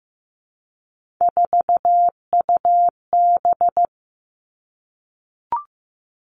Morse code, heard as a keyed tone, about 15 wpm, 700 Hz. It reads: 4UB
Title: Three Challenging Letter-Number Combinations 15wpm